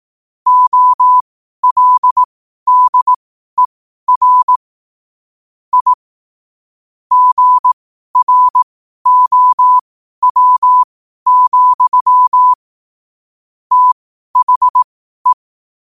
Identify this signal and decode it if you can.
Morse code, heard as a keyed tone, about 18 wpm, spaced as for 12 wpm, 1000 Hz. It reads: OLDER I GROW, THE
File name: qotdfiles / 12wpm